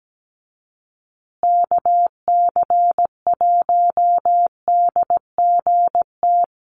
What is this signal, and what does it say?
Morse code, heard as a keyed tone, about 17 wpm, 700 Hz. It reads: KC1DGT
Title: USA Call Signs Repeated 3x 17wpm